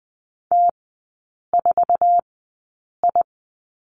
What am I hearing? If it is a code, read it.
Morse code, heard as a keyed tone, about 20 wpm, spaced as for 8 wpm, 700 Hz. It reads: T4I
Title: 060 Lesson 11e - Letter h - Three Characters 20wpm